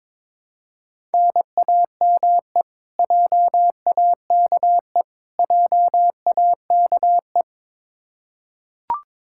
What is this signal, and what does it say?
Morse code, heard as a keyed tone, about 21 wpm, 700 Hz. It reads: NAME JAKE JAKE
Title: Practice QSO - Exchange 1 - Beginning - 22wpm